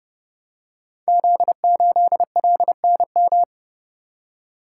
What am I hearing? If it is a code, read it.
Morse code, heard as a keyed tone, about 28 wpm, 700 Hz. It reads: Z8LNM
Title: Code Groups - 5 Letter:Numbers - 30wpm